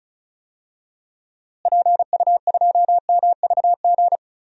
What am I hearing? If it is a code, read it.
Morse code, heard as a keyed tone, about 34 wpm, 700 Hz. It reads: PU2MVG
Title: DXCC-100 Call Signs - 6 Characters - Repeated 2x - 35wpm